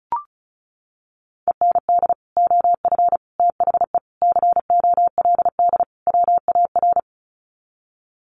Morse code, heard as a keyed tone, about 34 wpm, 700 Hz. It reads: END OF THE COLD WAR
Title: Sets of 5 Words Encore 35wpm